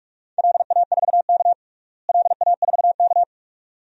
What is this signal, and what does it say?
Morse code, heard as a keyed tone, about 47 wpm, 700 Hz. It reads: LA4K LA4K
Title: DXCC-100 Call Signs - 4 Characters - Repeated 2x - 45wpm